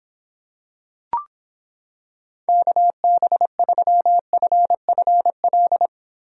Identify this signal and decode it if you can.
Morse code, heard as a keyed tone, about 26 wpm, 700 Hz. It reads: KB3FFL